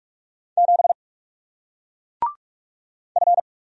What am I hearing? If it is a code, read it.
Morse code, heard as a keyed tone, about 38 wpm, 700 Hz. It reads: 7 F